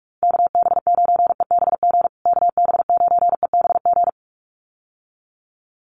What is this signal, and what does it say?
Morse code, heard as a keyed tone, about 44 wpm, 700 Hz. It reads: KB9EBG KB9EBG
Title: USA Call Signs Repeated 2x 45wpm